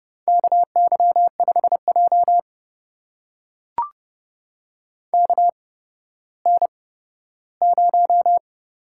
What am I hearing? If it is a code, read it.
Morse code, heard as a keyed tone, about 31 wpm, spaced as for 15 wpm, 700 Hz. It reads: KY5J KN0
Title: USA Call Signs - 4 Characters - 8x Character Spacing - 30wpm